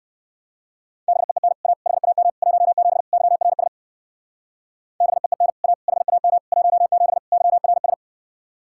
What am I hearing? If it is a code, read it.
Morse code, heard as a keyed tone, about 56 wpm, 700 Hz. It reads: BEEN A HAM 17 YRS BEEN A HAM 17 YRS
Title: Speed-Racing Common QSO Phrases - 45wpm